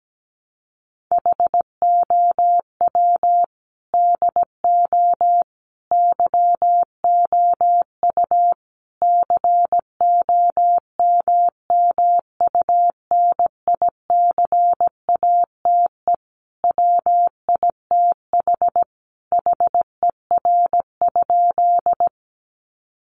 Morse code, heard as a keyed tone, about 17 wpm, 700 Hz. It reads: HOW DO YOU COMMUNICATE WITH HER?
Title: Sentences from Top 1300 Words 17wpm